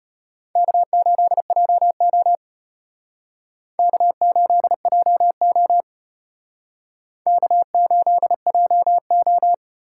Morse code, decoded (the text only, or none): K8JO K8JO K8JO